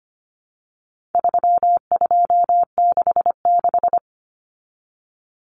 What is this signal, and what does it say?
Morse code, heard as a keyed tone, about 25 wpm, 700 Hz. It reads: 3266
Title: Numbers - 4-Digit 25wpm